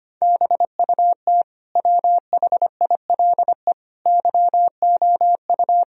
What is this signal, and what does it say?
Morse code, heard as a keyed tone, about 25 wpm, 700 Hz. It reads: BUT WHILE YOU
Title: Sentences from Top 1000 Words - Review Part 1 - 25wpm